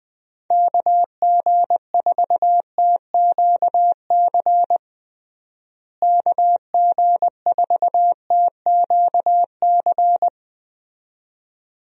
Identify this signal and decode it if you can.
Morse code, heard as a keyed tone, about 21 wpm, 700 Hz. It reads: KG4TQC KG4TQC
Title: USA Call Signs - 6 Characters - Repeated 2x - 20wpm